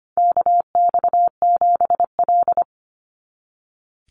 Morse code, heard as a keyed tone, about 25 wpm, 700 Hz. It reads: KX7L
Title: NA SOTA Chaser Callsigns 25wpm